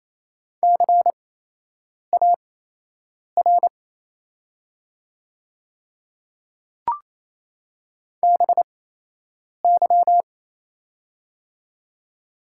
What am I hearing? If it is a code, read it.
Morse code, heard as a keyed tone, about 28 wpm, spaced as for 8 wpm, 700 Hz. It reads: CAR BY
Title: 8x Character Spacing - Top 500 Words - 28wpm